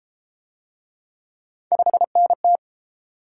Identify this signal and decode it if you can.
Morse code, heard as a keyed tone, about 31 wpm, 700 Hz. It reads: 5NT